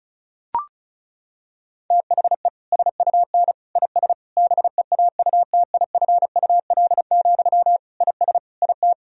Morse code, heard as a keyed tone, about 36 wpm, 700 Hz. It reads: THE SUN IS BEAUTIFUL, IS IT
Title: Sentences from Top 1000 Words - Review Part 1 - 35wpm